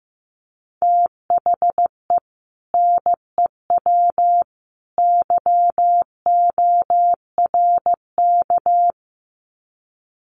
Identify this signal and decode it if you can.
Morse code, heard as a keyed tone, about 15 wpm, 700 Hz. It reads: THE NEW YORK
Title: Sets of 3 Words 15wpm